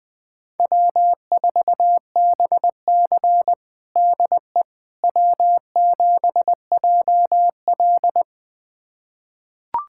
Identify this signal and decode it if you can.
Morse code, heard as a keyed tone, about 20 wpm, 700 Hz. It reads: W4BC DE W7JL
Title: USA Callsign DE Callsign - 4 Characters - 20wpm